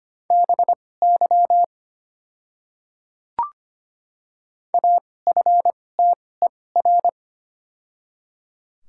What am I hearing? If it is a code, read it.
Morse code, heard as a keyed tone, about 25 wpm, spaced as for 18 wpm, 690 Hz. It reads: BY AFTER
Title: 2x Character Spacing - Top 200 Words - 25wpm